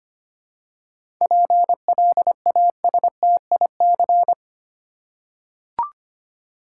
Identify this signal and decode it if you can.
Morse code, heard as a keyed tone, about 25 wpm, 700 Hz. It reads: PLASTIC
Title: Seven Letter Words - 25wpm